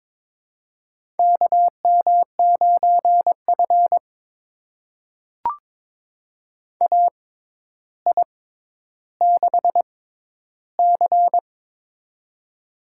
Morse code, heard as a keyed tone, about 22 wpm, spaced as for 11 wpm, 700 Hz. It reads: KM9F AI6C
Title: USA Call Signs - 4 Characters - 6x Character Spacing - 22wpm